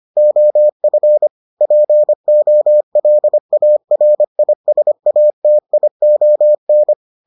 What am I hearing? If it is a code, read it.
Morse code, heard as a keyed tone, about 25 wpm, 600 Hz. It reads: OF POLARISATION